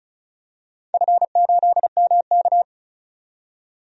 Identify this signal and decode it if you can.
Morse code, heard as a keyed tone, about 36 wpm, 700 Hz. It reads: F8MK
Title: DXCC-100 Call Signs - 4 Characters - 35wpm